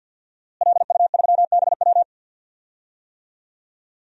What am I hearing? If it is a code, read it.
Morse code, heard as a keyed tone, about 46 wpm, 700 Hz. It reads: LU3BW
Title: DXCC-100 Call Signs - 5 Characters - Repeated 3x - 50wpm